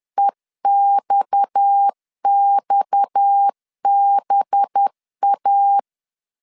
Morse code, heard as a keyed tone, about 10 wpm, 800 Hz. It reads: EXXBA
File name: LEZIONI / 6 PKX0 / 6 PKX 10wpm 2426